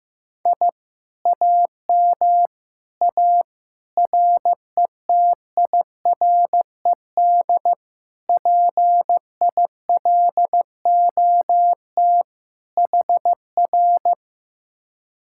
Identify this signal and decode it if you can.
Morse code, heard as a keyed tone, about 15 wpm, 700 Hz. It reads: I AM A RETIRED PILOT HR
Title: Speed-Racing Common QSO Phrases Encore - 15wpm